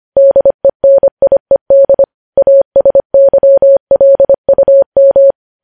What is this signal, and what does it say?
Morse code, heard as a keyed tone, about 25 wpm, 550 Hz. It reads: DENIED ASYLUM
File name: podcast / News headlines-1x-25x25